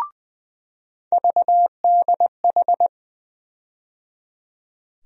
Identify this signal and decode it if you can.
Morse code, heard as a keyed tone, about 20 wpm, 700 Hz. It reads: VDH
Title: USA Call Sign Postfixes - 3 Letters 20wpm